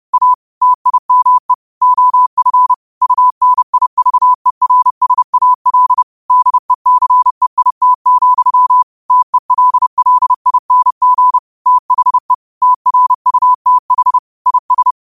Morse code, heard as a keyed tone, about 30 wpm, 1000 Hz. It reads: A TIME OF UNIVERSAL DECEIT, TELLING THE TRUTH IS